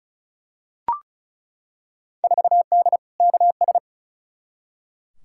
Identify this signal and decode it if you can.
Morse code, heard as a keyed tone, about 36 wpm, 700 Hz. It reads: 4D KS
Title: Field Day Exchanges Encore - 35wpm